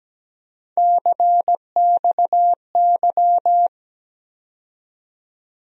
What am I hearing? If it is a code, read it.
Morse code, heard as a keyed tone, about 17 wpm, 700 Hz. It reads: CXY
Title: USA Call Sign Postfixes - 3 Letters 17wpm